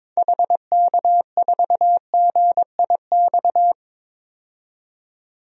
Morse code, heard as a keyed tone, about 22 wpm, 700 Hz. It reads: HK4GIX
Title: DXCC-100 Call Signs - 6 Characters - 22wpm